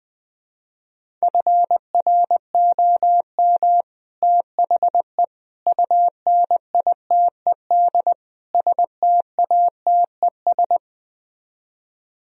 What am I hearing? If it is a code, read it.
Morse code, heard as a keyed tone, about 20 wpm, 700 Hz. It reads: FROM THE UNITED STATES
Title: Speed-Racing - Sets of 4 Words 17wpm